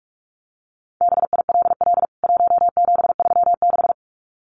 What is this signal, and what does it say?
Morse code, heard as a keyed tone, about 44 wpm, 700 Hz. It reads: BILL 1736
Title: CWT Exchange 45wpm